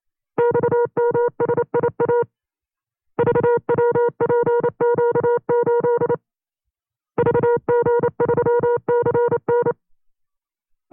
Morse code, heard as a keyed tone, about 29 wpm, 480 Hz. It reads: XMSIA VWPQ8 VG3CN
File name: CW-practice/25WPM